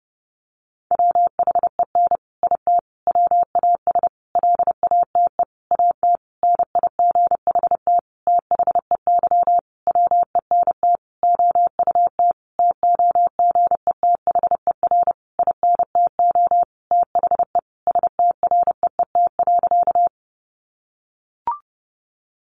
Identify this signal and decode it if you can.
Morse code, heard as a keyed tone, about 28 wpm, 700 Hz. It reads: WHEN IT WAS LATE AT NIGHT THEY WENT OUT TOGETHER INTO THE STREET.
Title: Sentences from Top 600 Words 30wpm